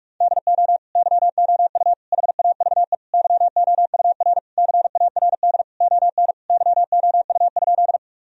Morse code, heard as a keyed tone, about 46 wpm, 700 Hz. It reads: DO YOU HAVE YOUR CARD ON YOU?
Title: Sentences from Top 1000 Words - Review Part 4 - 45wpm